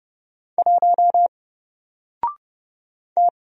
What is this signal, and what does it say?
Morse code, heard as a keyed tone, about 30 wpm, 700 Hz. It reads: 1 T
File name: Single-Letter-Number-30wpm